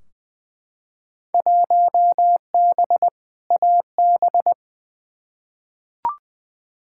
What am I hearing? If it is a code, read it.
Morse code, heard as a keyed tone, about 20 wpm, 700 Hz. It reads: 1B AB
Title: Field Day Exchanges - 20wpm